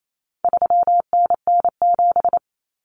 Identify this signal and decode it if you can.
Morse code, heard as a keyed tone, about 28 wpm, 700 Hz. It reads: 3NN7
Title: Common Cut Numbers - 4-Digit 28wpm